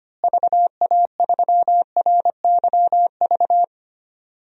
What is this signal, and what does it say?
Morse code, heard as a keyed tone, about 25 wpm, 700 Hz. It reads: VA3RYV
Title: DXCC-100 Call Signs - 6 Characters - Repeated 2x - 25wpm